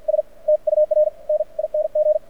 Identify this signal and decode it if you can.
Morse code, heard as a keyed tone, about 53 wpm, spaced as for 40 wpm, 590 Hz. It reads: S TURNING
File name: archived / CRSnet